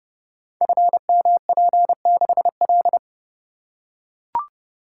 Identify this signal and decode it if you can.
Morse code, heard as a keyed tone, about 30 wpm, 700 Hz. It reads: FMP6L
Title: Code Groups - 5 Letter:Numbers - 30wpm